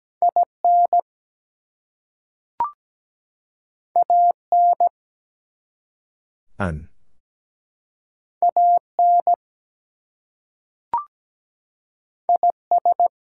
Morse code, heard as a keyed tone, about 17 wpm, 700 Hz. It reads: IN AN AN IS
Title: Two Letter Words - 17wpm